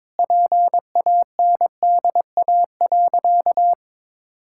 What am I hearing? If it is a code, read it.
Morse code, heard as a keyed tone, about 22 wpm, 700 Hz. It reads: PANDA.
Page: morse-code-ninja-practice-22wpm